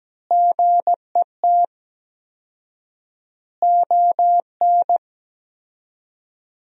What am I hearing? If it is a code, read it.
Morse code, heard as a keyed tone, about 17 wpm, 700 Hz. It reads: GET ON
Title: 4x Word-Spacing - Sentences from Top 200 Words - 17wpm